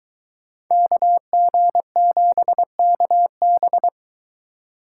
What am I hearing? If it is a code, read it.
Morse code, heard as a keyed tone, about 22 wpm, 700 Hz. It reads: KG7KB